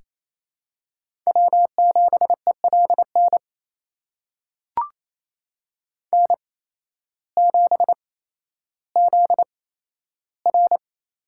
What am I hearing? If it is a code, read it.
Morse code, heard as a keyed tone, about 28 wpm, spaced as for 13 wpm, 700 Hz. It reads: W7ELN N7ZR
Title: USA Call Signs - 5 Characters - 8x Character Spacing - 28wpm